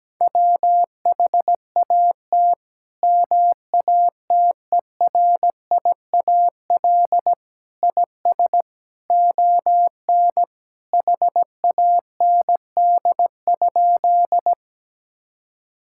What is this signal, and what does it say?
Morse code, heard as a keyed tone, about 17 wpm, 700 Hz. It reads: WHAT MATERIAL IS ON HAND?
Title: Sentences from Top 1000 Words - Review Part 5 - 17wpm